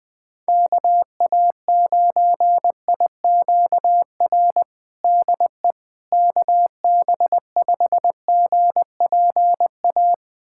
Morse code, heard as a keyed tone, about 20 wpm, 700 Hz. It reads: KA9IQR DE KB5GPA
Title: USA Callsign DE Callsign - 6 Characters - 20wpm